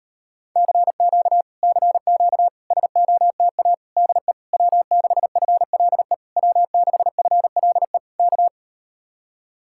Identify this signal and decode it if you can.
Morse code, heard as a keyed tone, about 39 wpm, 700 Hz. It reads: CQ CQ SOTA DE W6FLE W6FLE K